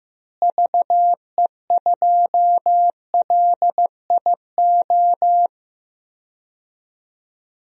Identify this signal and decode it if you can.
Morse code, heard as a keyed tone, about 15 wpm, 700 Hz. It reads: VE2LIO
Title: DXCC-100 Call Signs - 6 Characters - Repeated 2x - 15wpm